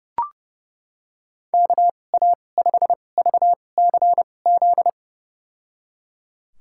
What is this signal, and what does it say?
Morse code, heard as a keyed tone, about 30 wpm, spaced as for 22 wpm, 700 Hz. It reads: KA5VCZ
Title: USA Call Signs - 6 Characters - 2x Character Spacing - 30wpm